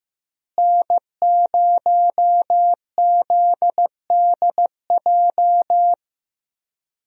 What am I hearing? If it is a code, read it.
Morse code, heard as a keyed tone, about 15 wpm, 700 Hz. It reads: N0ZDJ
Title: USA Call Signs - 5 Characters - Repeated 2x - 15wpm